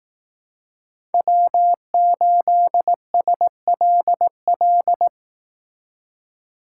Morse code, heard as a keyed tone, about 19 wpm, 700 Hz. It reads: W8SLL